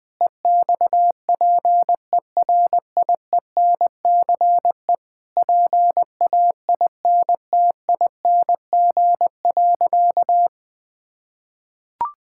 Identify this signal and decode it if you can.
Morse code, heard as a keyed tone, about 20 wpm, 700 Hz. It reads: EXPERIENCE PAINTING.
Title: Sentences from Top 1000 Words - Review Part 6 - 20wpm